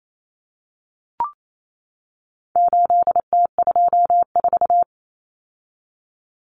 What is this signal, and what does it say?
Morse code, heard as a keyed tone, about 28 wpm, 700 Hz. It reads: 8T24
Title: Common Cut Numbers - 4-Digit 28wpm